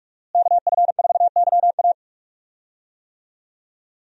Morse code, heard as a keyed tone, about 41 wpm, 700 Hz. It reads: KF4YA
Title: USA Call Signs - 5 Characters - Repeated 3x - 45wpm